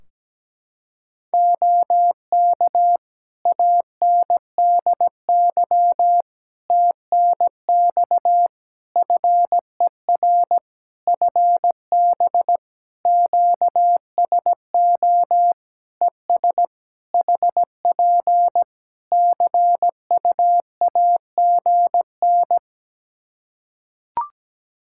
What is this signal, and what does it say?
Morse code, heard as a keyed tone, about 17 wpm, 700 Hz. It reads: OK ANDY TNX FER FB QSO ES HP CUAGN
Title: K1USN SST QSO Practice 17wpm